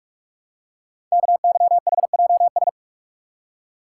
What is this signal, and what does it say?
Morse code, heard as a keyed tone, about 44 wpm, 700 Hz. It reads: KYHJS
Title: Code Groups - 5 Letters - 45wpm